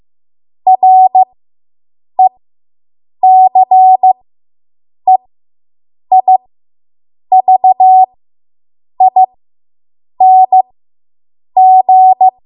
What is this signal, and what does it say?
Morse code, heard as a keyed tone, about 15 wpm, spaced as for 7 wpm, 750 Hz. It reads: RECEIVING